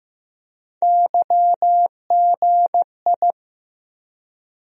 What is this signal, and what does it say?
Morse code, heard as a keyed tone, about 15 wpm, 700 Hz. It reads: YGI
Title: USA Call Sign Postfixes - 3 Letters 15wpm